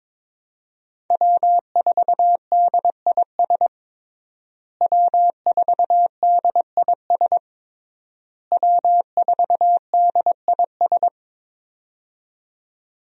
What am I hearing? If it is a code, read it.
Morse code, heard as a keyed tone, about 22 wpm, 700 Hz. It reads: W4DIS W4DIS W4DIS
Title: USA Call Signs - 5 Characters - Repeated 3x - 22wpm